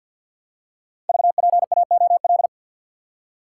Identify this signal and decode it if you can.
Morse code, heard as a keyed tone, about 50 wpm, 700 Hz. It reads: VPAOL
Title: Code Groups - 5 Letters - 50wpm